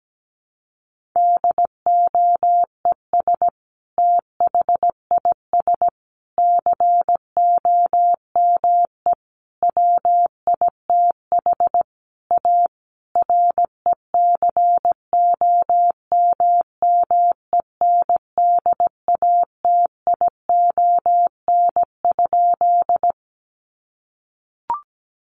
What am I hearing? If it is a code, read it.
Morse code, heard as a keyed tone, about 17 wpm, 700 Hz. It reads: DOES THIS COME WITH A RECOMMENDATION?
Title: Sentences from Top 1400 Words 17wpm